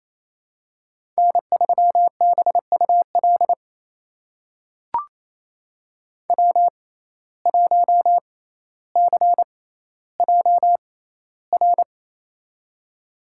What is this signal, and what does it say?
Morse code, heard as a keyed tone, about 28 wpm, spaced as for 14 wpm, 700 Hz. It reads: N3BUL W1CJR
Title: USA Call Signs - 5 Characters - 6x Character Spacing - 28wpm